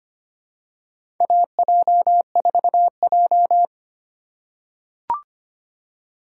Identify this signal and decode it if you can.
Morse code, heard as a keyed tone, about 25 wpm, 700 Hz. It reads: AJ4J
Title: USA Call Signs - 4 Characters - 4x Character Spacing - 25wpm